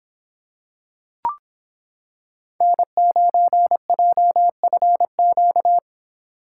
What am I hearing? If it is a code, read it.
Morse code, heard as a keyed tone, about 26 wpm, 700 Hz. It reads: N9JFQ